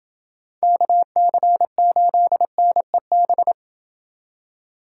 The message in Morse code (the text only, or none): KC8NEB